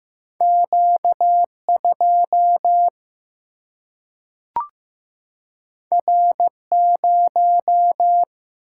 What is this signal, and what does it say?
Morse code, heard as a keyed tone, about 15 wpm, 700 Hz. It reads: Q2 R0
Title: Single-Letter Single-Number 15wpm